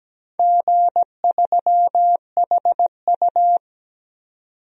Morse code, heard as a keyed tone, about 17 wpm, 700 Hz. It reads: G3HU
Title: DXCC-100 Call Signs - 4 Characters - 17wpm